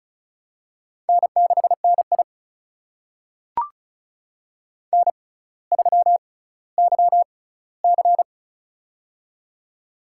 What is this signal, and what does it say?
Morse code, heard as a keyed tone, about 34 wpm, spaced as for 18 wpm, 700 Hz. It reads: N6NI N3YC
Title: USA Call Signs - 4 Characters - 6x Character Spacing - 35wpm